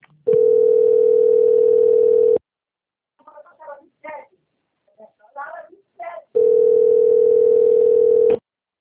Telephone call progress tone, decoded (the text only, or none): ringback tone